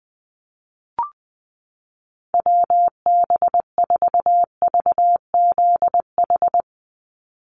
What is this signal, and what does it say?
Morse code, heard as a keyed tone, about 20 wpm, 700 Hz. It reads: WB4VZH
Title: USA Call Signs 20wpm